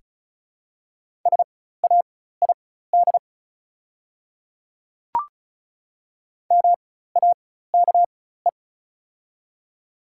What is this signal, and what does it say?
Morse code, heard as a keyed tone, about 36 wpm, spaced as for 16 wpm, 700 Hz. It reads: SAID MAKE
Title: 4x Character Spacing - Top 100 Words - 35wpm